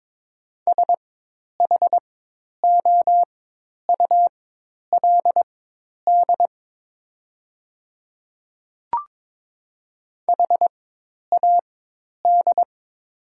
Morse code, heard as a keyed tone, about 22 wpm, spaced as for 10 wpm, 700 Hz. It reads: SHOULD HAD
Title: 4x Character Spacing - Top 500 Words - 22wpm